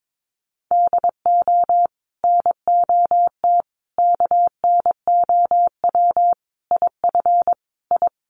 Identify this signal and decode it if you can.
Morse code, heard as a keyed tone, about 22 wpm, 700 Hz. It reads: DO NOT KNOW IF I
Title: Sets of 5 Words 22wpm